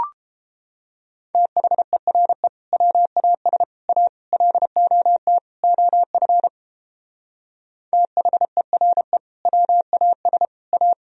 Morse code, heard as a keyed tone, about 31 wpm, 700 Hz. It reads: THERE WAS A LOT OF THERE WAS A